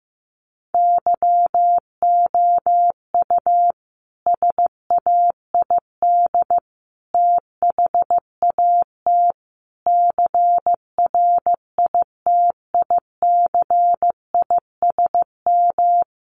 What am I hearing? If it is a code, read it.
Morse code, heard as a keyed tone, about 15 wpm, 700 Hz. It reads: YOU SAID THAT CRITICISM
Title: Sentences from Top 1300 Words 15wpm